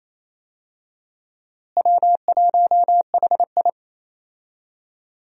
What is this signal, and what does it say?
Morse code, heard as a keyed tone, about 28 wpm, 700 Hz. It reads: W1HI
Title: USA Call Signs - 4 Characters - Repeated 3x - 28wpm